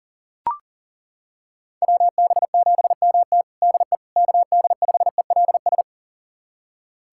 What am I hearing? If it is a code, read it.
Morse code, heard as a keyed tone, about 40 wpm, 700 Hz. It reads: WB7MT DE KD5ELS
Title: USA Callsign DE Callsign 40wpm